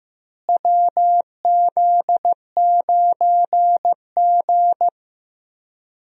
Morse code, heard as a keyed tone, about 15 wpm, 700 Hz. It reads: WZ9G
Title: USA Call Signs - 4 Characters - Repeated 3x - 15wpm